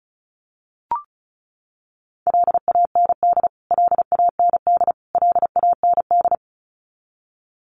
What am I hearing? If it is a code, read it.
Morse code, heard as a keyed tone, about 34 wpm, 700 Hz. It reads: LAND LAND LAND
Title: Top 200 Words Repeated 35wpm